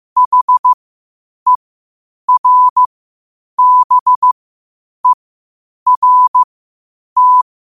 Morse code, heard as a keyed tone, about 15 wpm, spaced as for 8 wpm, 1000 Hz. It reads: HERBERT
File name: qotdfiles / 8wpm